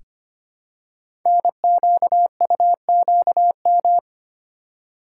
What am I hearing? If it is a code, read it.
Morse code, heard as a keyed tone, about 25 wpm, 700 Hz. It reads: NQUQM
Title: Code Groups - 5 Letters - 25wpm